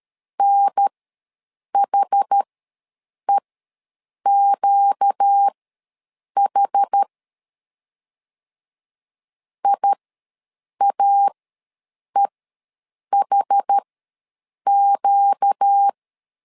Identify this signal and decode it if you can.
Morse code, heard as a keyed tone, about 13 wpm, spaced as for 7 wpm, 800 Hz. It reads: NHEQH IAEHQ
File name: LEZIONI / 4 Q ws 12wpm 1912